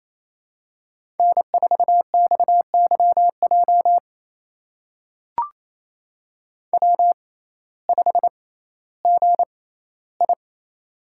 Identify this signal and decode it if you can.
Morse code, heard as a keyed tone, about 28 wpm, spaced as for 15 wpm, 700 Hz. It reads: N4XYJ W5GI
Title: USA Call Signs - 6x Character Spacing - 28wpm